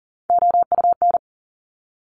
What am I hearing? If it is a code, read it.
Morse code, heard as a keyed tone, about 40 wpm, 690 Hz. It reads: OUN
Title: Top Three Letter Combinations in Words 40wpm